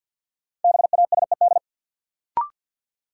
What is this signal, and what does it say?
Morse code, heard as a keyed tone, about 53 wpm, 700 Hz. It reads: BASED
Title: Sets of 2 Words - New Word Drill - 50wpm